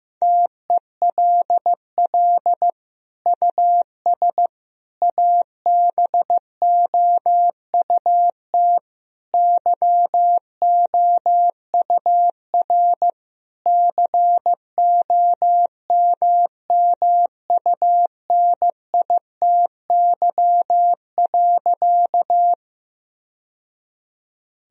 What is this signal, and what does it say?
Morse code, heard as a keyed tone, about 15 wpm, 700 Hz. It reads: TELL US ABOUT YOUR COMMUNITY.
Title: Sentences from Top 1000 Words - Review Part 4 - 15wpm